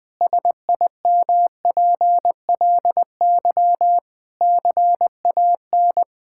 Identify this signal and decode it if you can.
Morse code, heard as a keyed tone, about 20 wpm, 700 Hz. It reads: SIMPLY CAN
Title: Difficult Sentences with Hints - 20wpm